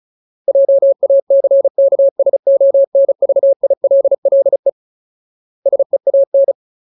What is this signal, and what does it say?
Morse code, heard as a keyed tone, about 34 wpm, 550 Hz. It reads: JACKSONVILLE SEAN
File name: Ultimate Word List / 35WPM Ultimate